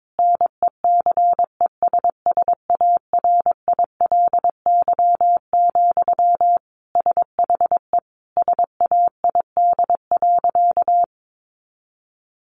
Morse code, heard as a keyed tone, about 22 wpm, 700 Hz. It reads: NECESSARILY, SHE SAID.
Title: Sentences from Top 1400 Words 22wpm